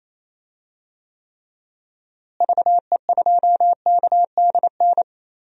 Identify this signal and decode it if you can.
Morse code, heard as a keyed tone, about 28 wpm, 700 Hz. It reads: VE2KDN
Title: DXCC-100 Call Signs - 6 Characters - Repeated 3x - 28wpm